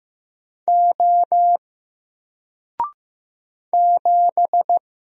Morse code, heard as a keyed tone, about 15 wpm, 700 Hz. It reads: O 7
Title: Single Letter-Number Alt 15wpm